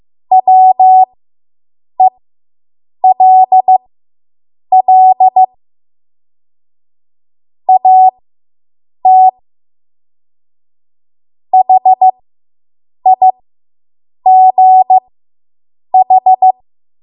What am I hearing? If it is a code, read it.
Morse code, heard as a keyed tone, about 15 wpm, spaced as for 7 wpm, 750 Hz. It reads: WELL AT HIGH